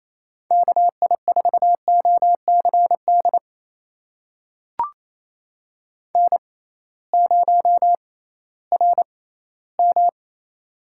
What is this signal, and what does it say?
Morse code, heard as a keyed tone, about 28 wpm, spaced as for 16 wpm, 700 Hz. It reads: KI4OCD N0RM
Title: USA Call Signs - 6x Character Spacing - 28wpm